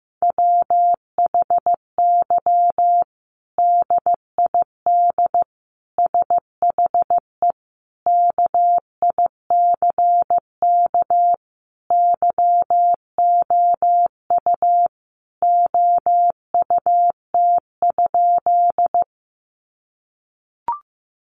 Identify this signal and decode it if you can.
Morse code, heard as a keyed tone, about 14 wpm, 700 Hz. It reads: WHY DID SHE KICK YOU OUT?
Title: Sentences from Top 1300 Words 15wpm